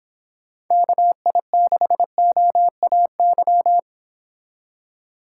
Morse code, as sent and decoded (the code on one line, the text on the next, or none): -.- .. -.... --- .- -.--
KI6OAY